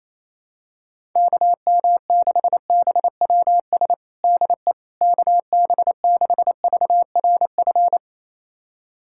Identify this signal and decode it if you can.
Morse code, heard as a keyed tone, about 28 wpm, 700 Hz. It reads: KM6BWS DE KB6VRF
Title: USA Callsign DE Callsign - 6 Characters - 28wpm